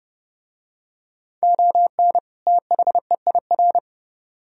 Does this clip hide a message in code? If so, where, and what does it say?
Yes: Morse code from 1.4 to 3.8 s, heard as a keyed tone, about 28 wpm, 700 Hz. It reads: ON THEIR